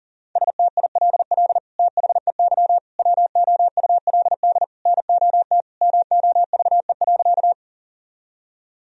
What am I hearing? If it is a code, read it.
Morse code, heard as a keyed tone, about 37 wpm, 700 Hz. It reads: STILL THEY WOULD NOT MOVE.
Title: Sentences from Top 1000 Words - Review Part 1 - 40wpm